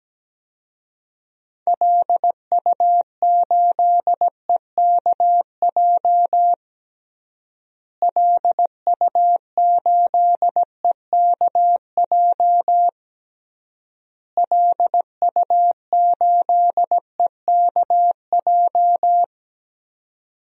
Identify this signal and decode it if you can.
Morse code, heard as a keyed tone, about 17 wpm, 700 Hz. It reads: LU8EKJ LU8EKJ LU8EKJ
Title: DXCC-100 Call Signs - 6 Characters - Repeated 3x - 17wpm